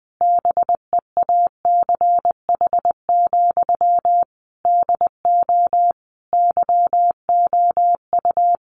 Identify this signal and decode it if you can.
Morse code, heard as a keyed tone, about 20 wpm, 700 Hz. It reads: BEACH, DO YOU
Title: Sentences from Top 1500 Words 20wpm